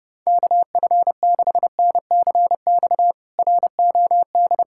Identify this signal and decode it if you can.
Morse code, heard as a keyed tone, about 29 wpm, 700 Hz. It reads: KF6NCX ROD
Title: K1USN SST QSO Practice - 30wpm